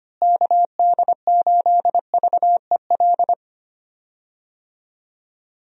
Morse code, heard as a keyed tone, about 25 wpm, 700 Hz. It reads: KD8VEL
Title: USA Call Signs - 6 Characters - Repeated 3x - 25wpm